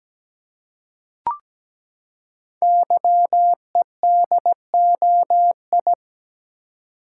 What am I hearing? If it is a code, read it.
Morse code, heard as a keyed tone, about 17 wpm, 700 Hz. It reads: YEDOI
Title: Code Groups - 5 Letters - 17wpm